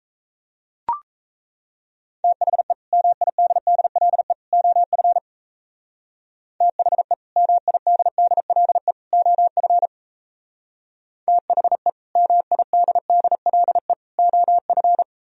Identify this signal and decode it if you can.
Morse code, heard as a keyed tone, about 37 wpm, 700 Hz. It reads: THE MIDDLE OF THE MIDDLE OF THE MIDDLE OF